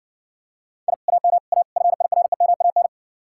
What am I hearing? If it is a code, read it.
Morse code, heard as a keyed tone, about 60 wpm, 700 Hz. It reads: I AM A FIREMAN